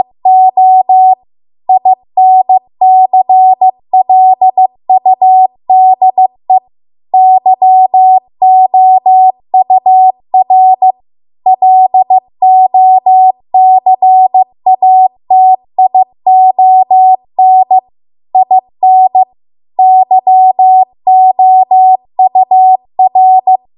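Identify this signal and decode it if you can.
Morse code, heard as a keyed tone, about 15 wpm, 750 Hz. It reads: O INCLUDE YOUR LOCATION IN YOUR
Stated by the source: Morse code practice at 15 wpm from the national amateur radio association's code archive.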